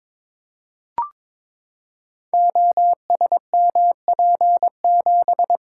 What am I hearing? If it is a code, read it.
Morse code, heard as a keyed tone, about 22 wpm, 700 Hz. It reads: OSMP7
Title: Code Groups - 5 Letter:Numbers - 22wpm